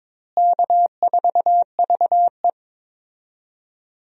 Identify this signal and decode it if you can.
Morse code, heard as a keyed tone, about 22 wpm, 700 Hz. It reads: K4VE
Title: USA Call Signs - 4 Characters - Repeated 2x - 22wpm